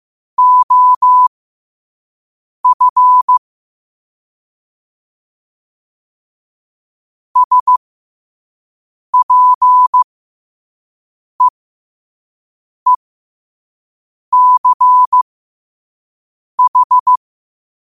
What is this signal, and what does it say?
Morse code, heard as a keyed tone, about 15 wpm, spaced as for 5 wpm, 1000 Hz. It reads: OF SPEECH